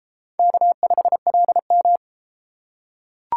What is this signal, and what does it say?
Morse code, heard as a keyed tone, about 32 wpm, 700 Hz. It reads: K5LM